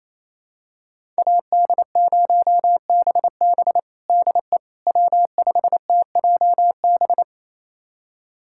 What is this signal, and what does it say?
Morse code, heard as a keyed tone, about 28 wpm, 700 Hz. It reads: AD0BB DE W5TJB
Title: USA Callsign DE Callsign - 5 Characters - 28wpm